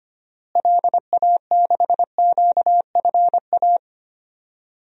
Morse code, heard as a keyed tone, about 25 wpm, 700 Hz. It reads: LA6QFA
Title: DXCC-100 Call Signs - 6 Characters - Repeated 2x - 25wpm